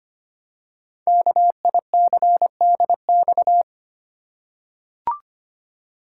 Morse code, heard as a keyed tone, about 25 wpm, 700 Hz. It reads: KICDX
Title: Code Groups - 5 Letter:Numbers - 25wpm